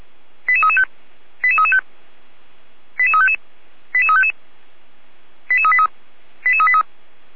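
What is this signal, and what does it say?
Five-tone selective call (ZVEI), heard as quick runs of five tones, tones of about 80 ms. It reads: 88335 88335 88355 88355 88383 88383
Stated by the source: ZVEI five-tone selective call received on an SDR.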